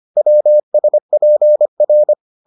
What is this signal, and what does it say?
Morse code, heard as a keyed tone, about 25 wpm, 600 Hz. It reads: WSPR